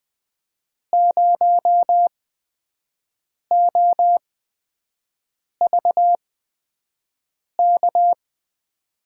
Morse code, heard as a keyed tone, about 20 wpm, spaced as for 5 wpm, 700 Hz. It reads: 0OVK
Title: USA Call Signs - 5 Characters - 8x Character Spacing - 20wpm